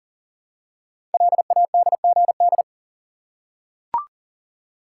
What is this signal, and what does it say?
Morse code, heard as a keyed tone, about 41 wpm, 700 Hz. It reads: LADGD
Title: Code Groups - 5 Letters - 40wpm